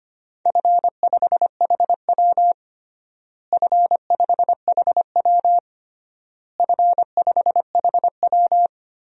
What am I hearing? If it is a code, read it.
Morse code, heard as a keyed tone, about 25 wpm, 700 Hz. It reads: F5HW F5HW F5HW
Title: DXCC-100 Call Signs - 4 Characters - Repeated 3x - 25wpm